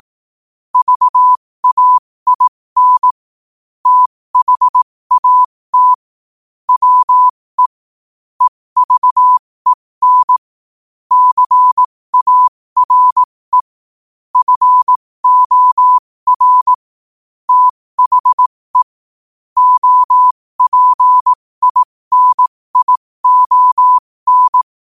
Morse code, heard as a keyed tone, about 18 wpm, spaced as for 15 wpm, 1000 Hz. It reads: VAIN THAT WE EVEN CARE FOR THE OPINION
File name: qotdfiles / 15wpm